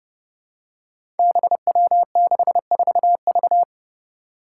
Morse code, heard as a keyed tone, about 32 wpm, 700 Hz. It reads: BW64V
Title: Five Challenging Letter-Number Combinations 30wpm